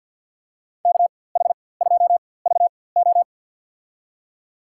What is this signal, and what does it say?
Morse code, heard as a keyed tone, about 53 wpm, spaced as for 24 wpm, 700 Hz. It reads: KH2VO
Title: USA Call Signs - 5 Characters - 4x Character Spacing - 50wpm